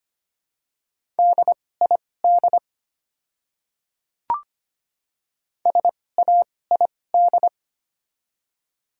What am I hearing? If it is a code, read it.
Morse code, heard as a keyed tone, about 25 wpm, spaced as for 18 wpm, 690 Hz. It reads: DID SAID
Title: 2x Character Spacing - Top 100 Words - 25wpm